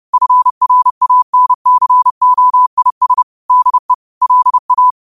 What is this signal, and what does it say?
Morse code, heard as a keyed tone, about 30 wpm, 1000 Hz. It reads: FRANCOIS DE LA
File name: qotdfiles / 30wpm